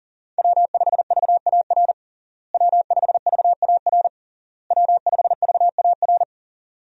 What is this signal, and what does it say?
Morse code, heard as a keyed tone, about 40 wpm, 700 Hz. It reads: W5VAR W5VAR W5VAR
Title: USA Call Signs - 5 Characters - Repeated 3x - 40wpm